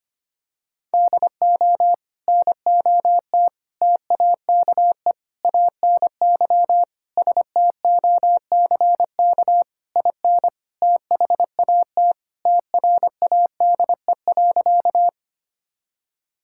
Morse code, heard as a keyed tone, about 25 wpm, 700 Hz. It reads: DO NOT TAKE ANY STOCK IN THAT TRADE.
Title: Sentences from Top 1000 Words - Review Part 5 - 25wpm